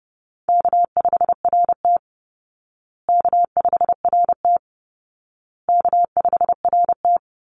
Morse code, heard as a keyed tone, about 30 wpm, 700 Hz. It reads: K5RT K5RT K5RT
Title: Easy Call Signs Repeated 3x - 30wpm